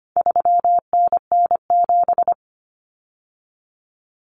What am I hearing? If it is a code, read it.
Morse code, heard as a keyed tone, about 25 wpm, 700 Hz. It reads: 3NN7
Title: Common Cut Numbers - 4-Digit 25wpm